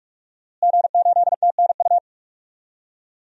Morse code, heard as a keyed tone, about 46 wpm, 700 Hz. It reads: G8TNU
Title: DXCC-100 Call Signs - 5 Characters - Repeated 3x - 45wpm